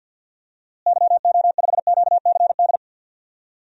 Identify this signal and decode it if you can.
Morse code, heard as a keyed tone, about 49 wpm, 700 Hz. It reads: YO5QCD
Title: DXCC-100 Call Signs - 6 Characters - Repeated 2x - 50wpm